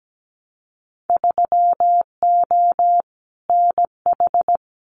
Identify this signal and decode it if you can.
Morse code, heard as a keyed tone, about 17 wpm, 700 Hz. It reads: 3O NH
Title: Winter Field Day Exchanges Encore 17wpm